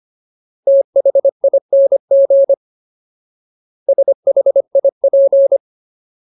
Morse code, heard as a keyed tone, about 25 wpm, 550 Hz. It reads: THING SHIP
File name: Ultimate Word List / 25WPM Ultimate